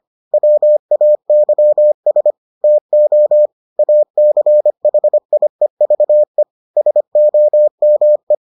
Morse code, heard as a keyed tone, about 25 wpm, 600 Hz. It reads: WAYS TO ACHIEVE SOME